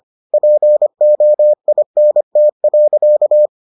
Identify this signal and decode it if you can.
Morse code, heard as a keyed tone, about 25 wpm, 600 Hz. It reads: POINT.